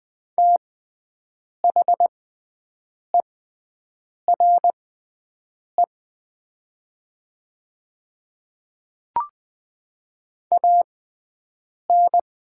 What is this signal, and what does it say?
Morse code, heard as a keyed tone, about 20 wpm, spaced as for 7 wpm, 700 Hz. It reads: THERE AN
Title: 6x Character Spacing - Top 100 Words - 20wpm